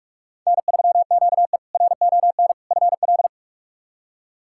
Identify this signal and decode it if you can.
Morse code, heard as a keyed tone, about 44 wpm, 700 Hz. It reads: N3QE RON FL
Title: K1USN SST QSO Practice - 45wpm